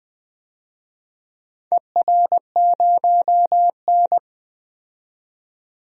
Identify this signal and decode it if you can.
Morse code, heard as a keyed tone, about 21 wpm, 700 Hz. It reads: ER0N
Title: DXCC-100 Call Signs - 4 Characters - 20wpm